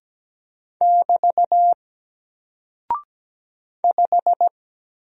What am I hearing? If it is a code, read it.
Morse code, heard as a keyed tone, about 17 wpm, 700 Hz. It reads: <BT> 5
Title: Single Character (Letter:Number:Punctuation:Prosign) Alt 17wpm